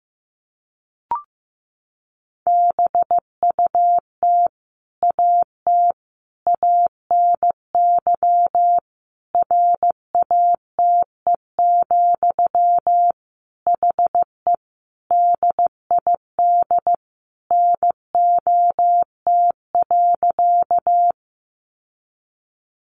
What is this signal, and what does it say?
Morse code, heard as a keyed tone, about 15 wpm, 700 Hz. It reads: BUT AT ANY RATE, HE DID NOT.
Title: Sentences from Top 700 Words 15wpm